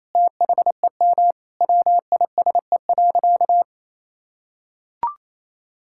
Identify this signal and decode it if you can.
Morse code, heard as a keyed tone, about 28 wpm, 700 Hz. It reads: THEM WISE.
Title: Sentences from Top 1600 Words 28wpm